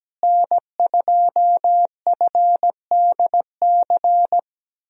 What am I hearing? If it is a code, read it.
Morse code, heard as a keyed tone, about 17 wpm, 700 Hz. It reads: N2FDC
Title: USA Call Signs - 5 Characters - Repeated 2x - 17wpm